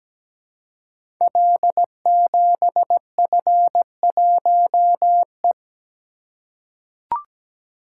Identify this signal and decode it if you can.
Morse code, heard as a keyed tone, about 17 wpm, 700 Hz. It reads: L7F1E
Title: Code Groups - 5 Letter:Numbers - 17wpm